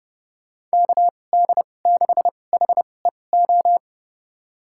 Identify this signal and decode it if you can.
Morse code, heard as a keyed tone, about 30 wpm, spaced as for 22 wpm, 700 Hz. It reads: KD6HEO
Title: USA Call Signs - 6 Characters - 2x Character Spacing - 30wpm